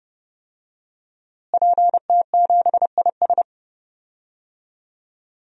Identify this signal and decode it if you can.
Morse code, heard as a keyed tone, about 30 wpm, 700 Hz. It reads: PT7IS
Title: DXCC-100 Call Signs - 5 Characters - Repeated 2x - 30wpm